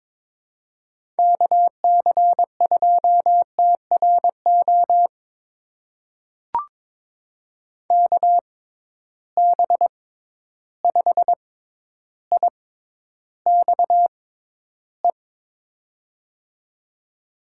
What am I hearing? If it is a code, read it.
Morse code, heard as a keyed tone, about 21 wpm, spaced as for 11 wpm, 700 Hz. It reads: KC2TRO KB5IXE
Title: USA Call Signs - 6 Characters - 6x Character Spacing - 22wpm